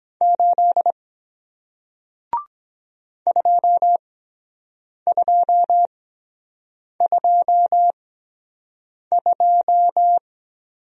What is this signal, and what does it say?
Morse code, heard as a keyed tone, about 23 wpm, spaced as for 7 wpm, 700 Hz. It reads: 8 2222